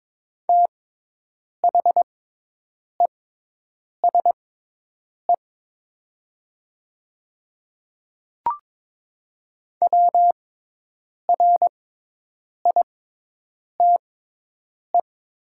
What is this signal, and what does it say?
Morse code, heard as a keyed tone, about 22 wpm, spaced as for 8 wpm, 700 Hz. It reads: THESE WRITE
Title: 6x Character Spacing - Top 200 Words - 22wpm